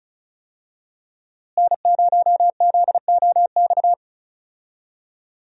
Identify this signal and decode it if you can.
Morse code, heard as a keyed tone, about 36 wpm, 700 Hz. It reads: N0ZOX
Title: USA Call Signs - 5 Characters - Repeated 2x - 35wpm